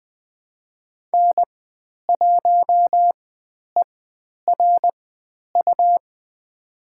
Morse code, heard as a keyed tone, about 20 wpm, spaced as for 10 wpm, 700 Hz. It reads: N1ERU
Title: 109 Lesson 18g - Letter m - Call Signs 20wpm